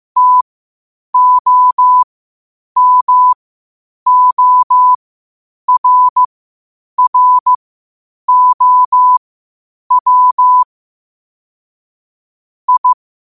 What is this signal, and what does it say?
Morse code, heard as a keyed tone, about 15 wpm, spaced as for 8 wpm, 1000 Hz. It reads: TOMORROW I
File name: qotdfiles / 8wpm